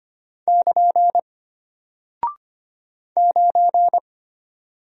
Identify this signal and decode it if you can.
Morse code, heard as a keyed tone, about 25 wpm, 700 Hz. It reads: <KN> 9
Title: Single Character (Letter:Number:Punctuation:Prosign) Alt 25wpm